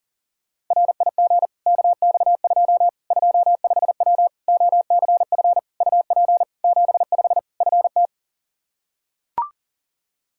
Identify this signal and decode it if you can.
Morse code, heard as a keyed tone, about 40 wpm, 700 Hz. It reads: RIG KX2 25W OCF UP 75 FT
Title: K1USN SST QSO Practice 40wpm